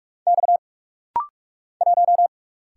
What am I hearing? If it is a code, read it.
Morse code, heard as a keyed tone, about 44 wpm, 700 Hz. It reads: X 1
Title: Single Letter-Number - Rapid-Fire 45wpm